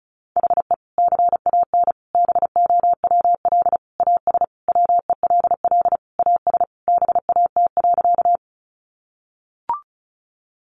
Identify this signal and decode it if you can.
Morse code, heard as a keyed tone, about 34 wpm, 700 Hz. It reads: HE CAN BOWL AS WELL AS BAT.
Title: Sentences from Top 1500 Words 35wpm